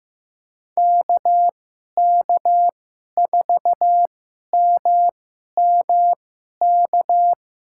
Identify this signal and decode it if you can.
Morse code, heard as a keyed tone, about 15 wpm, spaced as for 11 wpm, 700 Hz. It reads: KK4MMK
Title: USA Call Signs - 6 Characters - 2x Character Spacing - 15wpm